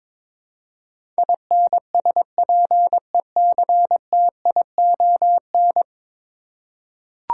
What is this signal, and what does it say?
Morse code, heard as a keyed tone, about 21 wpm, 700 Hz. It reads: INSPECTION
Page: morse-code-ninja-practice-22wpm